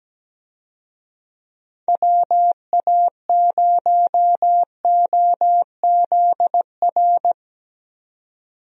Morse code, heard as a keyed tone, about 17 wpm, 700 Hz. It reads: WA0OZR
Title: USA Call Signs - 6 Characters - 17wpm